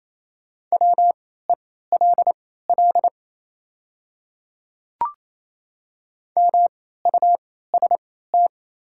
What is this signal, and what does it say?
Morse code, heard as a keyed tone, about 28 wpm, spaced as for 16 wpm, 700 Hz. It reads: WELL MUST
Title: 3x Character Spacing - Top 300 Words - 28wpm